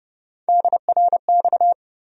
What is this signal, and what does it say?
Morse code, heard as a keyed tone, about 32 wpm, 700 Hz. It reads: DRX
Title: USA Call Sign Postfixes - 3 Letters 30wpm